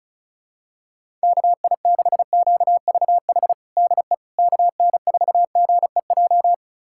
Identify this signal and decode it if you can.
Morse code, heard as a keyed tone, about 36 wpm, 700 Hz. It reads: KI6QVH DE KN4GEJ
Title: USA Callsign DE Callsign 35wpm